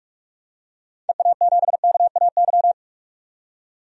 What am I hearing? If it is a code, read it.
Morse code, heard as a keyed tone, about 43 wpm, 700 Hz. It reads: EA7KAY
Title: DXCC-100 Call Signs - 6 Characters - 45wpm